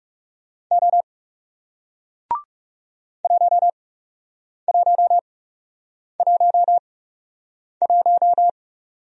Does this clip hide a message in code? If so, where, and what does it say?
Yes: Morse code from 0.7 to 8.5 s, heard as a keyed tone, about 41 wpm, spaced as for 8 wpm, 700 Hz. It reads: O 1111